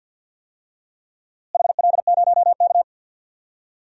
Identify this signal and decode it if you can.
Morse code, heard as a keyed tone, about 52 wpm, 700 Hz. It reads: HL0K
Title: DXCC-100 Call Signs - 4 Characters - Repeated 3x - 50wpm